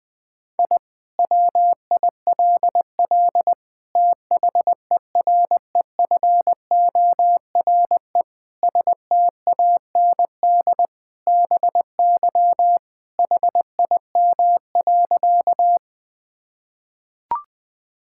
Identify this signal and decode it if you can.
Morse code, heard as a keyed tone, about 21 wpm, 700 Hz. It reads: I WILL THEREFORE STAND BY HIM.
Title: Sentences from Top 1000 Words - Review Part 5 - 20wpm